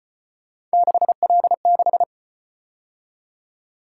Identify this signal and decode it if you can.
Morse code, heard as a keyed tone, about 33 wpm, 700 Hz. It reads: 6L6